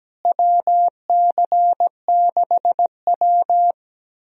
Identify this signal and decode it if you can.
Morse code, heard as a keyed tone, about 17 wpm, 700 Hz. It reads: WC6W
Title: USA Call Signs - 4 Characters - Repeated 2x - 17wpm